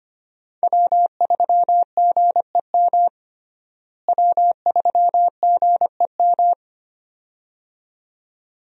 Morse code, heard as a keyed tone, about 25 wpm, 700 Hz. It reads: W3GEM W3GEM
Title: USA Call Signs - 5 Characters - Repeated 2x - 25wpm